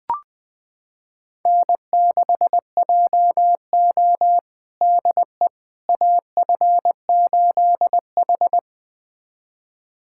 Morse code, heard as a keyed tone, about 20 wpm, 700 Hz. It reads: N6JO DE AF8H
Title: USA Callsign DE Callsign - 4 Characters - 20wpm